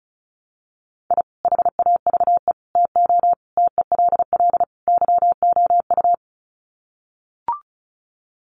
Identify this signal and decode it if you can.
Morse code, heard as a keyed tone, about 34 wpm, 700 Hz. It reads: I HAVE TO TELL YOU
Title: Sets of 5 Words 35wpm